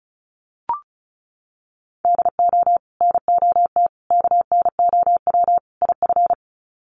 Morse code, heard as a keyed tone, about 34 wpm, 700 Hz. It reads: DO NOT KNOW IF
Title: Sets of 4 Words 35wpm